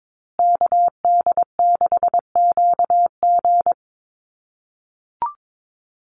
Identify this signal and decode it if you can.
Morse code, heard as a keyed tone, about 22 wpm, 700 Hz. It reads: KD6QG
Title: USA Call Signs Repeated 3x 22wpm